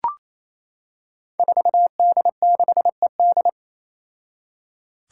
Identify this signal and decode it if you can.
Morse code, heard as a keyed tone, about 27 wpm, 700 Hz. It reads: 4D6ED
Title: Five Challenging Letter-Number Combinations 28wpm